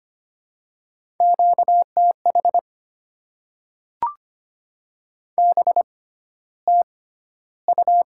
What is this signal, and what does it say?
Morse code, heard as a keyed tone, about 25 wpm, spaced as for 13 wpm, 700 Hz. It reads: QTH BTU
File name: Top 20 QSO Elements - 6x Character Spacing - 25wpm